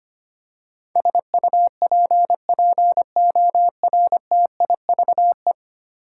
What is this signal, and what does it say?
Morse code, heard as a keyed tone, about 25 wpm, 700 Hz. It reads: SUPPORTIVE
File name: Ten Letter Words - 25wpm